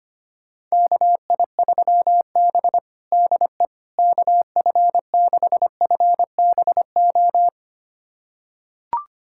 Morse code, heard as a keyed tone, about 25 wpm, 700 Hz. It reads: KI3B DE KF6FBO
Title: USA Callsign DE Callsign 25wpm